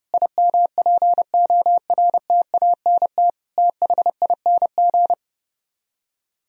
Morse code, heard as a keyed tone, about 32 wpm, 700 Hz. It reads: IMPORTANT THING